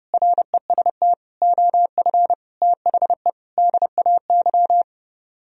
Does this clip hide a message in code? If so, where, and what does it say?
Yes: Morse code from 0.1 to 4.8 s, heard as a keyed tone, about 30 wpm, 700 Hz. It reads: REST OF THE DAY